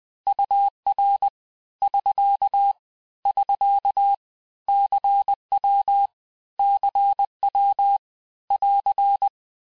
Morse code, heard as a keyed tone, about 20 wpm, 800 Hz. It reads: UR <SK> <SK> CW CW <AR>
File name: cours-du-lundi files / cours-1584-texte-abreviations-vitesse-19mpm-puis-24mpm